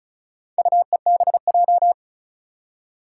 Morse code, heard as a keyed tone, about 36 wpm, 700 Hz. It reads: UEBJ
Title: Four Challenging Letter-Number Combinations 35wpm